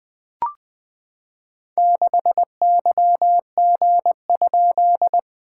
Morse code, heard as a keyed tone, about 20 wpm, 700 Hz. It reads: 6YG?
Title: SOTA QSO Practice 20wpm